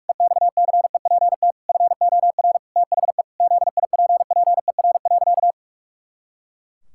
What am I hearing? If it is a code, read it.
Morse code, heard as a keyed tone, about 46 wpm, 700 Hz. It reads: EXCEPT FOR THE ZIPPER.
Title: Pangram Challenge v2 - 45wpm